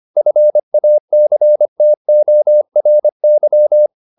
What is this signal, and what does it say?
Morse code, heard as a keyed tone, about 25 wpm, 600 Hz. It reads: FACTORY
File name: the old man and the sea part 1 25wpm